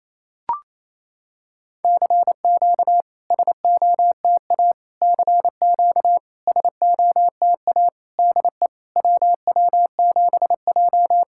Morse code, heard as a keyed tone, about 28 wpm, 690 Hz. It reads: CQ SOTA CQ SOTA DE WW7J
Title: SOTA QSO Practice 28wpm